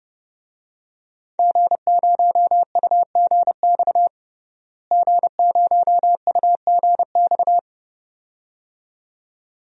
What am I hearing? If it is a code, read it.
Morse code, heard as a keyed tone, about 30 wpm, 700 Hz. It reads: G0UGX G0UGX
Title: DXCC-100 Call Signs - 5 Characters - Repeated 2x - 30wpm